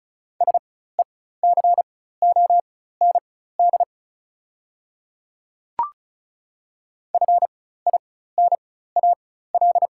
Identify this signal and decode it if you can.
Morse code, heard as a keyed tone, about 34 wpm, spaced as for 16 wpm, 690 Hz. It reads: SECOND FINAL
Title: 4x Character Spacing - Top 500 Words - 35wpm